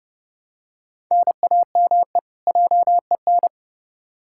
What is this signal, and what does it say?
Morse code, heard as a keyed tone, about 32 wpm, 700 Hz. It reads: NAME JEN
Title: K1USN SST QSO Practice 30wpm